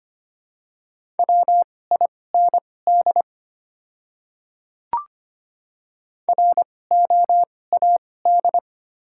Morse code, heard as a keyed tone, about 25 wpm, spaced as for 18 wpm, 690 Hz. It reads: WIND ROAD
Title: 2x Character Spacing - Top 500 Words - 25wpm